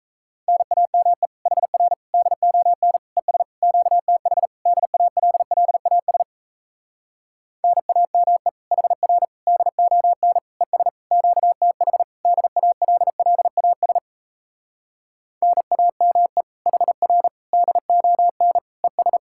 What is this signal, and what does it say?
Morse code, heard as a keyed tone, about 37 wpm, 700 Hz. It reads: NAME HR DON ES QTH DALLAS NAME HR DON ES QTH DALLAS NAME HR DON ES